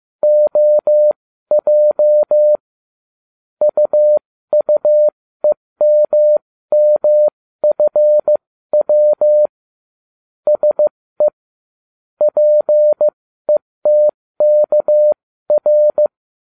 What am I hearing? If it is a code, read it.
Morse code, heard as a keyed tone, about 15 wpm, spaced as for 13 wpm, 600 Hz. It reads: OJ UUEMMFW SE PETKR